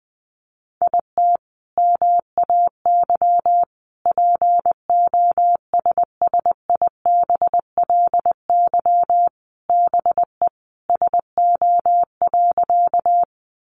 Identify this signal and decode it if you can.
Morse code, heard as a keyed tone, about 20 wpm, 700 Hz. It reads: IT MAY POSSIBLY BE SO.
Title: Sentences from Top 1300 Words 20wpm